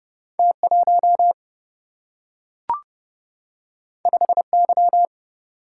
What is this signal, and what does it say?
Morse code, heard as a keyed tone, about 30 wpm, 700 Hz. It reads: T1 5Y
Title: Single-Letter Single-Number 30wpm